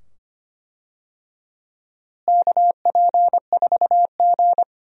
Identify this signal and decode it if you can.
Morse code, heard as a keyed tone, about 25 wpm, 700 Hz. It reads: KP4G
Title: DXCC-100 Call Signs - 4 Characters - Repeated 2x - 25wpm